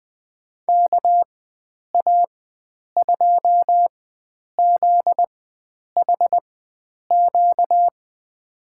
Morse code, heard as a keyed tone, about 20 wpm, spaced as for 9 wpm, 700 Hz. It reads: KA2ZHQ
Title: USA Call Signs - 4x Character Spacing - 20wpm